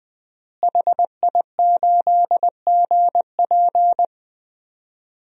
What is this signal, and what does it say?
Morse code, heard as a keyed tone, about 20 wpm, 700 Hz. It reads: HI8GP
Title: DXCC-100 Call Signs - 5 Characters - Repeated 2x - 20wpm